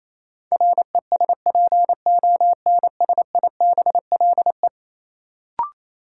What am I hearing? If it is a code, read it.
Morse code, heard as a keyed tone, about 28 wpm, 700 Hz. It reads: RESPONSIBLE
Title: Sentences from Top 1600 Words - New Word Drill Alt - 28wpm